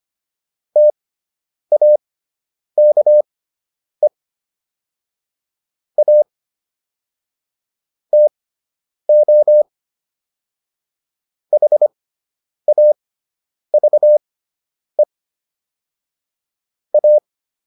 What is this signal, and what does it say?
Morse code, heard as a keyed tone, about 25 wpm, spaced as for 9 wpm, 600 Hz. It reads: TAKE A TO HAVE A